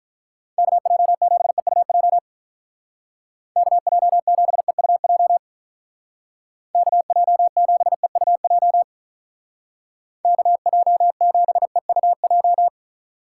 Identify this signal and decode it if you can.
Morse code, heard as a keyed tone, about 43 wpm, 700 Hz. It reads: KJ7EUJ KJ7EUJ KJ7EUJ KJ7EUJ
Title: Speed Racing - USA 2x3 Call Signs 35wpm